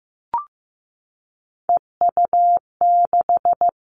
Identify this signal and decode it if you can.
Morse code, heard as a keyed tone, about 15 wpm, 700 Hz. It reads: EU6
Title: Callsign Prefixes for Top 100 DXCC - 15wpm